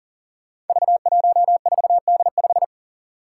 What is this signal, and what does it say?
Morse code, heard as a keyed tone, about 40 wpm, 700 Hz. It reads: V14D5
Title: Five Challenging Letter-Number Combinations 40wpm